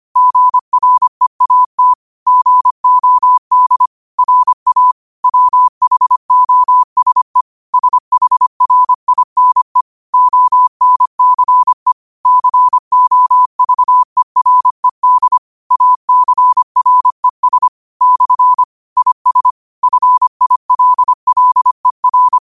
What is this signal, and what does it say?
Morse code, heard as a keyed tone, about 25 wpm, 1000 Hz. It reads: GREAT GOD RA WHOSE SHRINE ONCE COVERED ACRES / IS FILLER
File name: qotdfiles / 25wpm